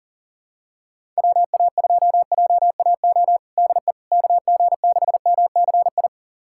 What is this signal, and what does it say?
Morse code, heard as a keyed tone, about 40 wpm, 700 Hz. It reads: WA2JAO DE KG6MCI
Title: USA Callsign DE Callsign 40wpm